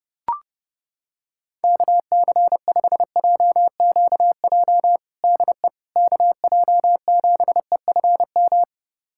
Morse code, heard as a keyed tone, about 30 wpm, 700 Hz. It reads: KC5JQJ DE KJ7EFM
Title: USA Callsign DE Callsign - 6 Characters - 30wpm